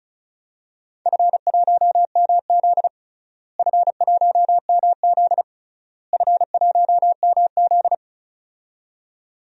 Morse code, heard as a keyed tone, about 34 wpm, 700 Hz. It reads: F1MZ F1MZ F1MZ
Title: DXCC-100 Call Signs - 4 Characters - Repeated 3x - 35wpm